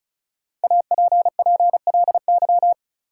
Morse code, heard as a keyed tone, about 36 wpm, 700 Hz. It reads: APPLY
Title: Sentences from Top 1200 Words - New Word Drill - 35wpm